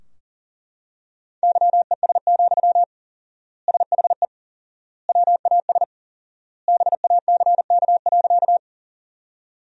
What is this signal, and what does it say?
Morse code, heard as a keyed tone, about 37 wpm, 700 Hz. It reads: YES, SHE WAS BACK.
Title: 4x Word-Spacing - Sentences from Top 500 Words - 40wpm